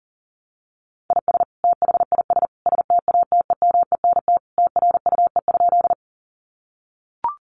Morse code, heard as a keyed tone, about 40 wpm, 700 Hz. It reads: IS THIS STATEMENT TRUE?
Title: Sentences from Top 1200 Words 40wpm